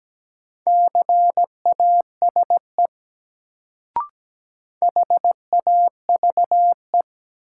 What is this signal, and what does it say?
Morse code, heard as a keyed tone, about 17 wpm, 700 Hz. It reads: CASE HAVE
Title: Sets of 2 Words - New Word Drill - 17wpm